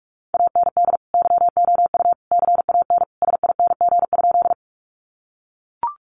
Morse code, heard as a keyed tone, about 44 wpm, 700 Hz. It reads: AND YOU CAN SING?
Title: Sentences from Top 400 Words 45wpm